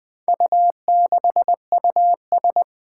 Morse code, heard as a keyed tone, about 20 wpm, 700 Hz. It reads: U6US
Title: Four Challenging Letter-Number Combinations 20wpm